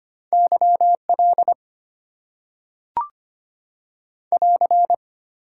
Morse code, heard as a keyed tone, about 25 wpm, 700 Hz. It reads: YL <AR>
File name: Top 90 QSO Elements - 8x Character Spacing - 25wpm